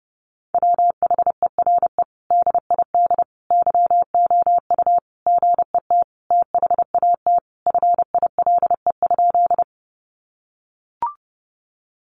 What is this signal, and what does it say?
Morse code, heard as a keyed tone, about 30 wpm, 700 Hz. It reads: WHERE DID YOU GET THAT FILE?
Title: Sentences from Top 1200 Words 30wpm